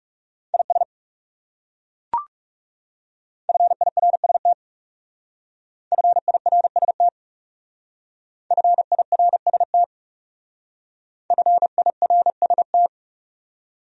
Morse code, heard as a keyed tone, about 37 wpm, 700 Hz. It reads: IS FIRST FIRST FIRST FIRST
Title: Speed-Racing Top 100 Words 30wpm